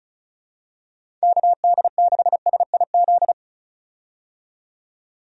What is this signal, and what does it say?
Morse code, heard as a keyed tone, about 34 wpm, 700 Hz. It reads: KD6SIZ
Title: USA Call Signs - 6 Characters - 35wpm